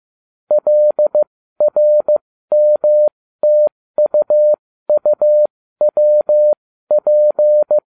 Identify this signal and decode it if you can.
Morse code, heard as a keyed tone, about 15 wpm, spaced as for 13 wpm, 600 Hz. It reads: LRMTUUWP